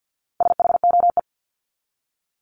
Morse code, heard as a keyed tone, about 50 wpm, 700 Hz. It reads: SHOE
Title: Top 1751-2000 Words 50wpm